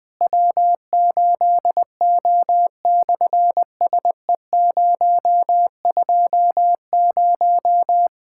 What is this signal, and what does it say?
Morse code, heard as a keyed tone, about 19 wpm, 700 Hz. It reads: W8O/SE020
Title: SOTA QSO Practice 20wpm